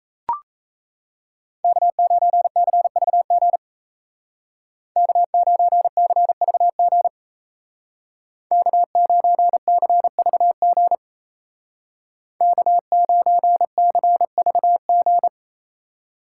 Morse code, heard as a keyed tone, about 37 wpm, 700 Hz. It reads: K9CVG K9CVG K9CVG K9CVG